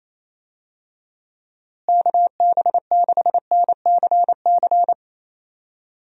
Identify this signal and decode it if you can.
Morse code, heard as a keyed tone, about 28 wpm, 700 Hz. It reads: KB6NCC
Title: USA Call Signs - 6 Characters - Repeated 2x - 28wpm